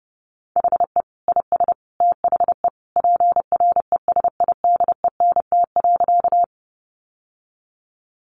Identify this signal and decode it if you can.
Morse code, heard as a keyed tone, about 30 wpm, 700 Hz. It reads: HE IS THE PRESIDENT.
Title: Sentences from Top 1000 Words 30wpm